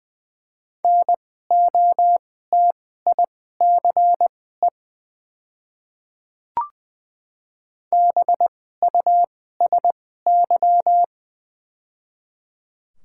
Morse code, heard as a keyed tone, about 20 wpm, spaced as for 14 wpm, 690 Hz. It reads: NOTICE BUSY
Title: 2x Character Spacing - Top 500 Words - 20wpm